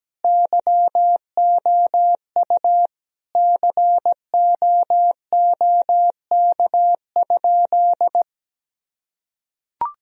Morse code, heard as a keyed tone, about 17 wpm, 700 Hz. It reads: YOU COOK?
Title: Sentences from Top 1000 Words - Review Part 4 - 17wpm